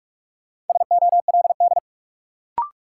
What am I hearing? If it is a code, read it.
Morse code, heard as a keyed tone, about 41 wpm, 700 Hz. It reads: SOLD
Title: Sentences from Top 1400 Words - New Word Drill - 45wpm